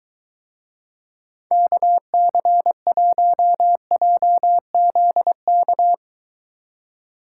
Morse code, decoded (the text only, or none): KC1JZK